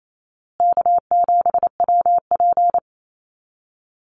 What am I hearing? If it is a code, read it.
Morse code, heard as a keyed tone, about 28 wpm, 700 Hz. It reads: K7WP
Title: Easy Call Signs Repeated 3x - 28wpm